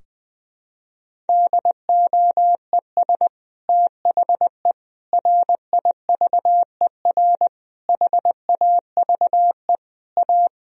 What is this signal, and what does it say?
Morse code, heard as a keyed tone, about 20 wpm, 700 Hz. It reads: DOES THE RIVER HAVE A
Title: Sentences from Top 1000 Words - Review Part 4 - 20wpm